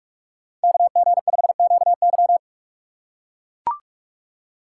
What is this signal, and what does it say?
Morse code, heard as a keyed tone, about 42 wpm, 700 Hz. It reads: KG5QY
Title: USA Call Signs - 5 Characters - 3x Character Spacing - 45wpm